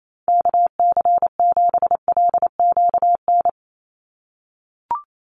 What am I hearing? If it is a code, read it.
Morse code, heard as a keyed tone, about 28 wpm, 700 Hz. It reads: KC7LQN
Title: USA Call Signs 28wpm